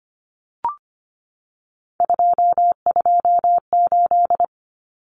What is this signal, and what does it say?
Morse code, heard as a keyed tone, about 25 wpm, 700 Hz. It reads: 228
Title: Common Cut Numbers - 3-Digit 25wpm